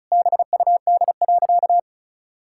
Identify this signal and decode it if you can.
Morse code, heard as a keyed tone, about 36 wpm, 700 Hz. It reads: BUD.
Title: Pangram Challenge v2 - 35wpm